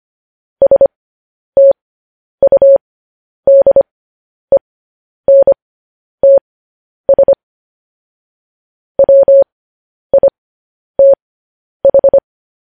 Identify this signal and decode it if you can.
Morse code, heard as a keyed tone, about 25 wpm, spaced as for 10 wpm, 550 Hz. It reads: STUDENTS WITH
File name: podcast / News headlines-1x-25x10